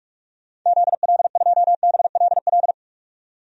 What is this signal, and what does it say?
Morse code, heard as a keyed tone, about 47 wpm, 700 Hz. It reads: ZL2BLL
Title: DXCC-100 Call Signs - 6 Characters - 45wpm